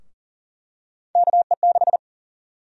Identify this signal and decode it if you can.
Morse code, heard as a keyed tone, about 40 wpm, 700 Hz. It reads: KE6
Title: USA Call Sign Prefixes - 40wpm